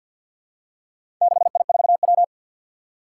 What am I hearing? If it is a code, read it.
Morse code, heard as a keyed tone, about 50 wpm, 700 Hz. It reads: 6I4W
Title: Four Challenging Letter-Number Combinations 50wpm